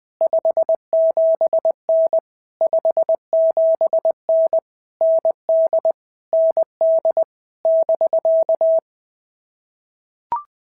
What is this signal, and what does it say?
Morse code, heard as a keyed tone, about 20 wpm, 650 Hz. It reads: 57N 57N ND ND <BK>
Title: POTA QSO Practice - 20wpm